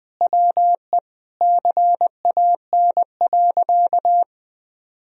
Morse code, heard as a keyed tone, about 21 wpm, 700 Hz. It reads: WE CAN.
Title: Sentences from Top 1000 Words - Review Part 3 - 20wpm